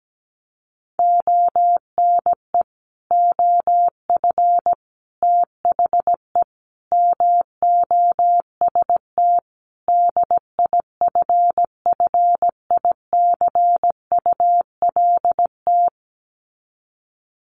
Morse code, heard as a keyed tone, about 17 wpm, 700 Hz. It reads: ONE OF THE MOST DIFFICULT
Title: Sets of 5 Words Encore2 17wpm